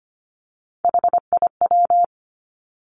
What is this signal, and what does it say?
Morse code, heard as a keyed tone, about 25 wpm, 700 Hz. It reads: HIW
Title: Three Challenging Letter-Number Combinations 25wpm